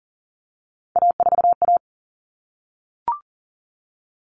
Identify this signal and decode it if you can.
Morse code, heard as a keyed tone, about 40 wpm, 700 Hz. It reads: A4A
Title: All Cut Numbers - 3-Digit 40wpm